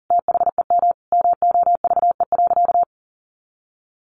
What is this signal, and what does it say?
Morse code, heard as a keyed tone, about 37 wpm, 700 Hz. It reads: THEM MOVE.
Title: A Sound Of Thunder 40wpm